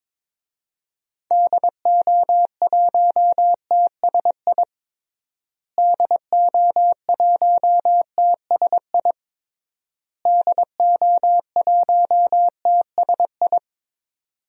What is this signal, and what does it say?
Morse code, heard as a keyed tone, about 22 wpm, 700 Hz. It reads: DO1TSI DO1TSI DO1TSI
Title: DXCC-100 Call Signs - 6 Characters - Repeated 3x - 22wpm